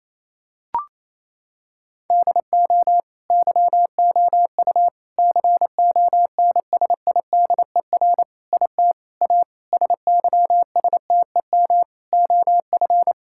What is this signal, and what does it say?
Morse code, heard as a keyed tone, about 28 wpm, 700 Hz. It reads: DO YOU CONSIDER IT A SYSTEM OF
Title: Sentences from Top 1000 Words - Review Part 6 - 28wpm